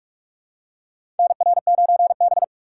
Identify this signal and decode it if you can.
Morse code, heard as a keyed tone, about 47 wpm, 700 Hz. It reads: NR9B
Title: USA Call Signs - 4 Characters - 3x Character Spacing - 45wpm